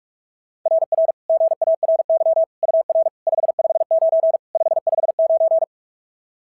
Morse code, heard as a keyed tone, about 44 wpm, 690 Hz. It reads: RR GARY UR 559 559
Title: SOTA QSO Practice 45wpm